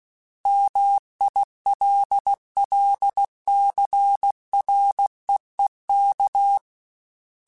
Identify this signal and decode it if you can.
Morse code, heard as a keyed tone, about 16 wpm, 800 Hz. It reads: MILLCREEK
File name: fromSMs / FromSMs mp3s / Cities@15